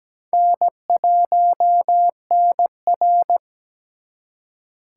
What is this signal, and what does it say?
Morse code, heard as a keyed tone, about 20 wpm, 700 Hz. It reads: N1NR
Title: Speed Racing - USA 1x2 Call Signs 17wpm